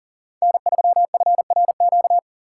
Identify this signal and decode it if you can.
Morse code, heard as a keyed tone, about 40 wpm, 700 Hz. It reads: N3FRQ
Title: USA Call Signs - 4x Character Spacing - 40wpm